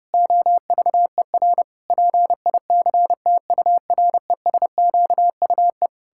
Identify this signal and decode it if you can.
Morse code, heard as a keyed tone, about 32 wpm, 700 Hz. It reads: OVER PICTURESQUE
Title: Pangram Challenge v2 - 30wpm